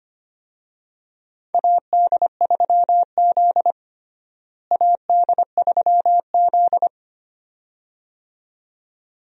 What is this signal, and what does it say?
Morse code, heard as a keyed tone, about 25 wpm, 700 Hz. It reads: AD3Z AD3Z
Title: USA Call Signs - 4 Characters - Repeated 2x - 25wpm